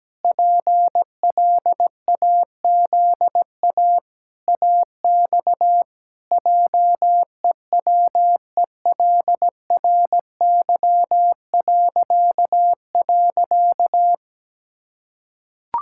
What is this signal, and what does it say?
Morse code, heard as a keyed tone, about 17 wpm, 700 Hz. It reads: PLAZA AX JEWELRY..
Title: Pangram Challenge v2 - 17wpm